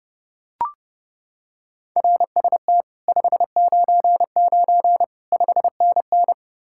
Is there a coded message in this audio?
Yes: Morse code from 2.0 to 6.3 s, heard as a keyed tone, about 29 wpm, 700 Hz. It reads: RST 599 5NN